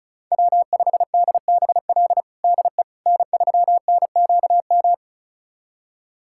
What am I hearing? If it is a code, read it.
Morse code, heard as a keyed tone, about 34 wpm, 700 Hz. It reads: W5DBL DE N3NQM
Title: USA Callsign DE Callsign - 5 Characters - 35wpm